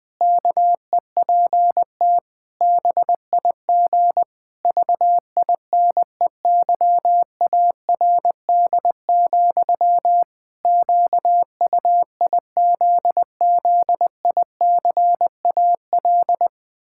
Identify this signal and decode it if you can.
Morse code, heard as a keyed tone, about 20 wpm, 700 Hz. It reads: KEPT BIG VINEYARD, QUIZZICAL
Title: Pangram Challenge v2 - 20wpm